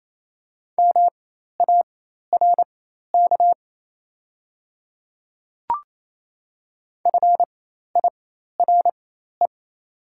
Morse code, heard as a keyed tone, about 28 wpm, spaced as for 13 wpm, 700 Hz. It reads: MARK FIRE
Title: 4x Character Spacing - Top 500 Words - 28wpm